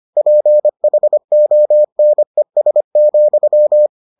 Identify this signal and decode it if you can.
Morse code, heard as a keyed tone, about 25 wpm, 600 Hz. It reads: PHONES,